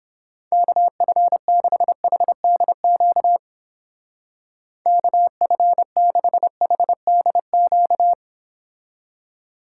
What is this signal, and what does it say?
Morse code, heard as a keyed tone, about 31 wpm, 700 Hz. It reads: KF6HDQ KF6HDQ